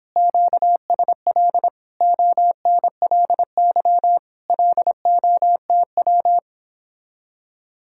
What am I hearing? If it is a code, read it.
Morse code, heard as a keyed tone, about 29 wpm, 700 Hz. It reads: QSL ONLY LOTW